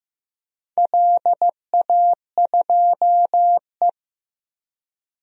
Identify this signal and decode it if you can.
Morse code, heard as a keyed tone, about 15 wpm, 700 Hz. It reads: LA2E
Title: DXCC-100 Call Signs - 4 Characters - Repeated 3x - 15wpm